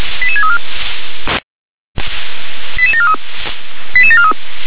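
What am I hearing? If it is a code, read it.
Five-tone selective call (ZVEI), heard as quick runs of five tones, tones of about 70 ms. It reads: 88634 88743 88743
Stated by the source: ZVEI five-tone call received by SDR.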